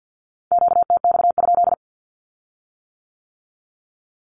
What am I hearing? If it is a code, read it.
Morse code, heard as a keyed tone, about 50 wpm, 700 Hz. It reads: QTX?
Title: Q-Codes with Spoken Meaning - 50wpm